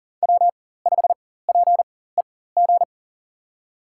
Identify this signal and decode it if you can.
Morse code, heard as a keyed tone, about 40 wpm, spaced as for 19 wpm, 700 Hz. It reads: W5PEG
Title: USA Call Signs - 5 Characters - 4x Character Spacing - 40wpm